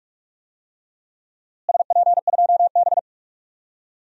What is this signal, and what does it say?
Morse code, heard as a keyed tone, about 44 wpm, 700 Hz. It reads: SP2B
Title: DXCC-100 Call Signs - 4 Characters - Repeated 2x - 45wpm